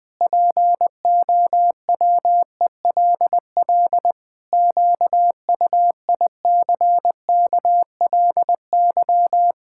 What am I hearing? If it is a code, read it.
Morse code, heard as a keyed tone, about 20 wpm, 700 Hz. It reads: POWELL QUICKLY
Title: Pangram Challenge v2 - 20wpm